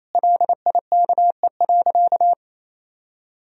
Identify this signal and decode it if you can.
Morse code, heard as a keyed tone, about 28 wpm, 700 Hz. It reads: LIKE.
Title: Sentences from Top 1000 Words - Review Part 3 - 28wpm